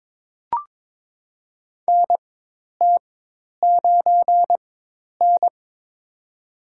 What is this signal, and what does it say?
Morse code, heard as a keyed tone, about 22 wpm, spaced as for 10 wpm, 700 Hz. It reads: NT9N
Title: USA Call Signs - 4 Characters - 4x Character Spacing - 22wpm